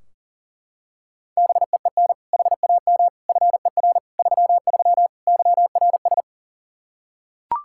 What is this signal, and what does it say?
Morse code, heard as a keyed tone, about 40 wpm, 700 Hz. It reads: BEEN HAM FER 33 YRS
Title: Practice QSO - Exchange 3 - More Info - 40wpm